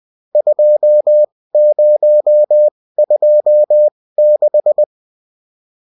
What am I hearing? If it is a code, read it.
Morse code, heard as a keyed tone, about 20 wpm, spaced as for 16 wpm, 600 Hz. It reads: 2026